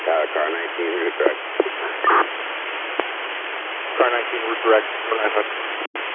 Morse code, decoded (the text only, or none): none